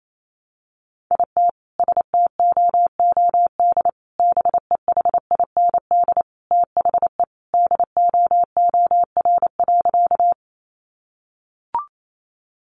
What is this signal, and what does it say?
Morse code, heard as a keyed tone, about 28 wpm, 700 Hz. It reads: IT STOOD BEHIND THE DOOR.
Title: Sentences from Top 500 Words 28wpm